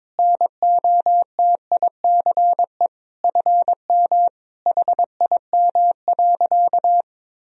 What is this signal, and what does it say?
Morse code, heard as a keyed tone, about 22 wpm, 700 Hz. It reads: NOTICE FM HIM.
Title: Sentences from Top 400 Words and Abbreviations 22wpm